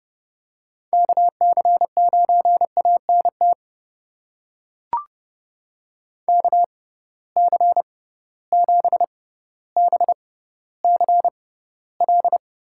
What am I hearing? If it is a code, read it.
Morse code, heard as a keyed tone, about 30 wpm, spaced as for 15 wpm, 700 Hz. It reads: KC9ANT KC7BCL
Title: USA Call Signs - 6 Characters - 6x Character Spacing - 30wpm